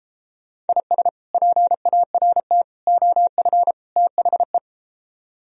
Morse code, heard as a keyed tone, about 34 wpm, 700 Hz. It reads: IS PART OF THE